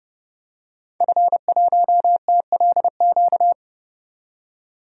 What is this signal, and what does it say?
Morse code, heard as a keyed tone, about 30 wpm, 700 Hz. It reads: F1TLQ
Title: DXCC-100 Call Signs - 5 Characters - 30wpm